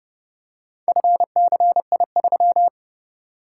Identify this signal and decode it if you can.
Morse code, heard as a keyed tone, about 30 wpm, 700 Hz. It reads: FCI3